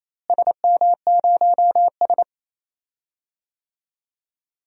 Morse code, heard as a keyed tone, about 28 wpm, 700 Hz. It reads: SM0S
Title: DXCC-100 Call Signs - 4 Characters - Repeated 2x - 28wpm